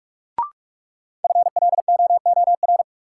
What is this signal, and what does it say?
Morse code, heard as a keyed tone, about 46 wpm, 700 Hz. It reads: FLOOR
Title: Sets of 3 Words - New Word Drill - 45wpm